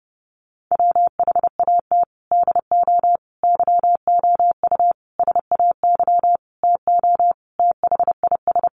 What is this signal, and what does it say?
Morse code, heard as a keyed tone, about 30 wpm, 700 Hz. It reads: WHAT DO YOU SAY TO THIS
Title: Sentences from Top 1400 Words 30wpm